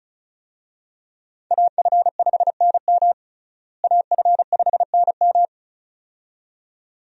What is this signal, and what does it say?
Morse code, heard as a keyed tone, about 34 wpm, 700 Hz. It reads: AF5NM AF5NM
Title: USA Call Signs - 5 Characters - Repeated 2x - 35wpm